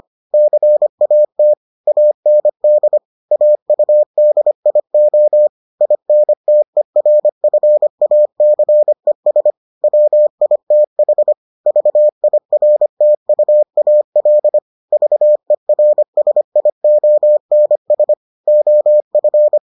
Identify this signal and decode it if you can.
Morse code, heard as a keyed tone, about 25 wpm, 600 Hz. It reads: CAT AND AUDIO INTERFACES WITH VIRTUAL VERSIONS OF